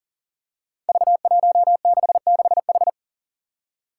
Morse code, heard as a keyed tone, about 41 wpm, 700 Hz. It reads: V166H
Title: Five Challenging Letter-Number Combinations 40wpm